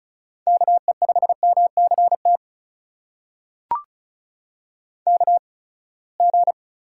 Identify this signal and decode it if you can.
Morse code, heard as a keyed tone, about 36 wpm, spaced as for 24 wpm, 700 Hz. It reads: KE5MCT KG
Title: USA Call Signs - 6 Characters - 8x Character Spacing - 35wpm